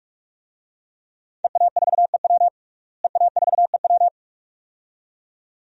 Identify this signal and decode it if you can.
Morse code, heard as a keyed tone, about 43 wpm, 700 Hz. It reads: EA4EW EA4EW
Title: DXCC-100 Call Signs - 5 Characters - Repeated 2x - 45wpm